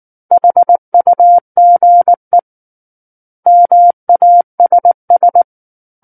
Morse code, heard as a keyed tone, about 19 wpm, 700 Hz. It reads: HUGE MASS
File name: Book One / 10-Chapter 10 - 19 WPM